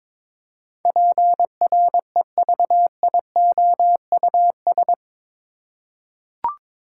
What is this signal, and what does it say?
Morse code, heard as a keyed tone, about 22 wpm, 700 Hz. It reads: PREVIOUS
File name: practice / 22wpm / Eight Letter Words - 22wpm